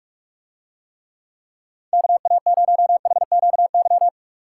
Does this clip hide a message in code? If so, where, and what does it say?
Yes: Morse code from 1.9 to 4.1 s, heard as a keyed tone, about 45 wpm, 700 Hz. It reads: KA0HQY